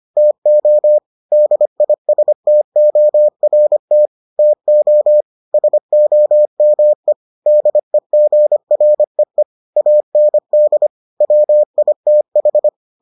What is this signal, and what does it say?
Morse code, heard as a keyed tone, about 25 wpm, 600 Hz. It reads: TO DISTORT TO SOME DEGREE AND WITH